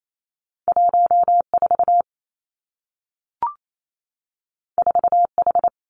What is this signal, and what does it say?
Morse code, heard as a keyed tone, about 28 wpm, 700 Hz. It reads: 1 4 4 H
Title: Two Challenging Letter-Number Combinations 28wpm